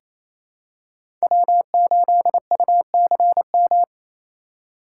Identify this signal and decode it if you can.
Morse code, heard as a keyed tone, about 27 wpm, 700 Hz. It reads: W8UCM
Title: USA Call Signs - 5 Characters - 4x Character Spacing - 28wpm